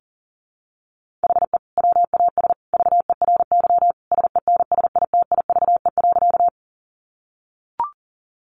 Morse code, heard as a keyed tone, about 40 wpm, 700 Hz. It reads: HE WAS VERY SENSITIVE.
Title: Sentences from Top 1400 Words 40wpm